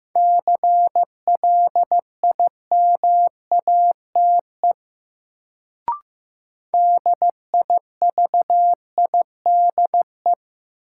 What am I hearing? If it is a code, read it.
Morse code, heard as a keyed tone, about 15 wpm, 700 Hz. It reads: CLIMATE DIVIDE
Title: Sentences from Top 1600 Words - New Word Drill Alt - 15wpm